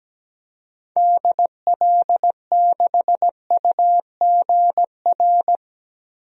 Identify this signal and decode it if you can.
Morse code, heard as a keyed tone, about 17 wpm, 700 Hz. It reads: DL6UGR
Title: DXCC-100 Call Signs - 6 Characters - Repeated 3x - 17wpm